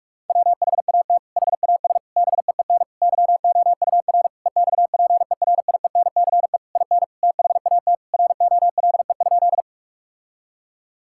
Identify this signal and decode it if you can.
Morse code, heard as a keyed tone, about 42 wpm, 700 Hz. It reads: WHAT HAS BEEN YOUR EXPERIENCE IN THAT ROLE?
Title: Sentences from Top 1000 Words - Review Part 6 - 45wpm